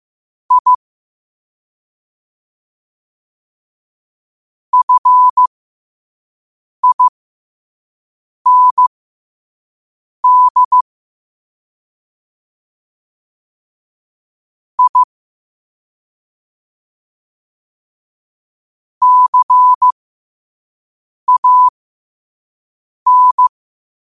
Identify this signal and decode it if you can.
Morse code, heard as a keyed tone, about 15 wpm, spaced as for 5 wpm, 1000 Hz. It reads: I FIND I CAN
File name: qotdfiles / 5wpm